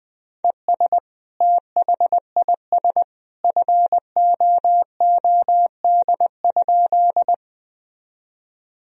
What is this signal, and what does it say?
Morse code, heard as a keyed tone, about 21 wpm, 700 Hz. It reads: ES THIS FOOD?
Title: Sentences from Top 300 Words and Abbreviations 20wpm